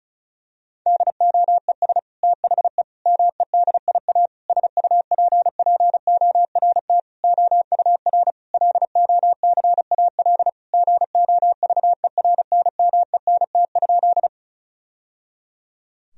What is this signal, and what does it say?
Morse code, heard as a keyed tone, about 34 wpm, 700 Hz. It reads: DOES THE MEDIA SUPPORT OUR LOCAL GOVERNMENT?
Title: Sentences from Top 1000 Words - Review Part 4 - 35wpm